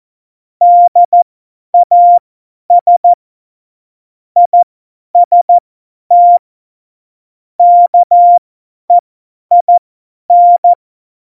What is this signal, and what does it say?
Morse code, heard as a keyed tone, about 14 wpm, spaced as for 10 wpm, 700 Hz. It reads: DAS IST KEIN
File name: Morsekurs / Witze / Witz 04. September 2018, 50 F70